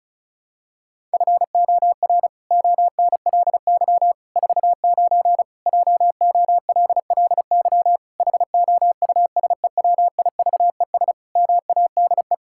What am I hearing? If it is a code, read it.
Morse code, heard as a keyed tone, about 36 wpm, 700 Hz. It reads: FOR ONLY 49 JOLLY HOUSEWIVES MADE
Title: Pangram Challenge v2 - 35wpm